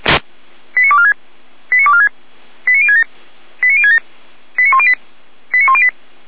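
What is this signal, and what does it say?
Five-tone selective call (ZVEI), heard as quick runs of five tones, tones of about 70 ms. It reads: 89246 89246 89067 89067 89108 89108
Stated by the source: ZVEI five-tone call received by SDR.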